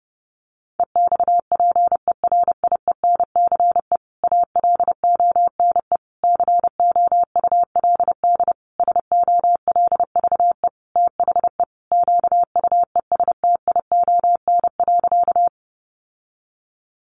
Morse code, heard as a keyed tone, about 30 wpm, 700 Hz. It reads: EXPERIENCE ALONE COULD SOLVE THE QUESTION.
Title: Sentences from Top 1000 Words 30wpm